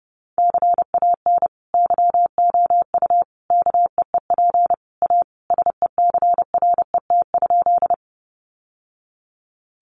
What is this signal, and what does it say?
Morse code, heard as a keyed tone, about 30 wpm, 700 Hz. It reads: CAN YOU KEEP A SECRET?
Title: Sentences from Top 1200 Words 30wpm